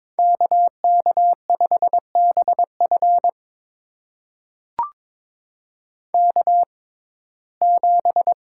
Morse code, heard as a keyed tone, about 22 wpm, spaced as for 16 wpm, 700 Hz. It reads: KK5BF K7
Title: USA Call Signs - 5 Characters - 6x Character Spacing - 22wpm